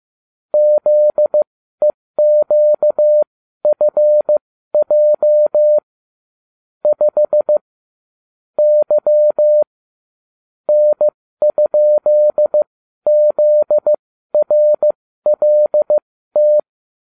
Morse code, heard as a keyed tone, about 15 wpm, spaced as for 12 wpm, 600 Hz. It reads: ZEQFJ 5 Y N?ZRLT